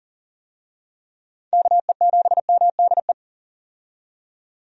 Morse code, heard as a keyed tone, about 40 wpm, 700 Hz. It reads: KE7MDE
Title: USA Call Signs - 6 Characters - 40wpm